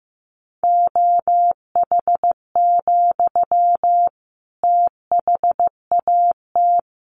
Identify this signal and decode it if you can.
Morse code, heard as a keyed tone, about 15 wpm, 700 Hz. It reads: OH, THAT
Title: Sentences from Top 1100 Words 15wpm